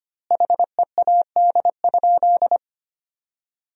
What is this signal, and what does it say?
Morse code, heard as a keyed tone, about 25 wpm, 700 Hz. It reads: HEAD?
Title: 2x Word-Spacing - Sentences from Top 300 Words - 25wpm